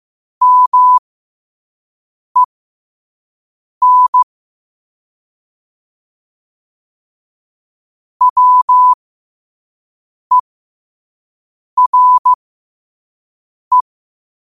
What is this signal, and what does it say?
Morse code, heard as a keyed tone, about 15 wpm, spaced as for 5 wpm, 1000 Hz. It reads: MEN WERE
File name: qotdfiles / 5wpm